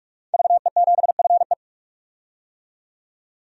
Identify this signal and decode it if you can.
Morse code, heard as a keyed tone, about 44 wpm, 700 Hz. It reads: VE7FE
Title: DXCC-100 Call Signs - 5 Characters - 45wpm